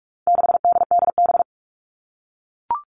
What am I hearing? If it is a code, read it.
Morse code, heard as a keyed tone, about 44 wpm, 700 Hz. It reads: 6DDB
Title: All Cut Numbers - 4-Digit 45wpm